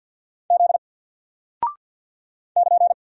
Morse code, heard as a keyed tone, about 48 wpm, 700 Hz. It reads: Z <KN>
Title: Single Character (Letter:Number:Punctuation:Prosign) Alt 50wpm